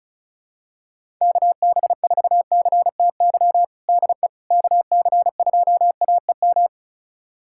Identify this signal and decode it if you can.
Morse code, heard as a keyed tone, about 36 wpm, 700 Hz. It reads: KB4CTY DE KC2AEM
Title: USA Callsign DE Callsign 35wpm